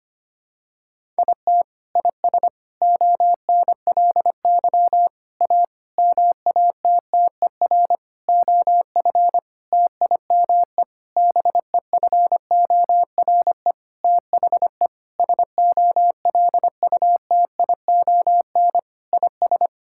Morse code, heard as a keyed tone, about 25 wpm, 700 Hz. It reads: IT IS ONLY A MATTER OF TIME BEFORE THE SOLUTION IS
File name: practice / 25wpm / Sentences with 9-Word Phrases - 25wpm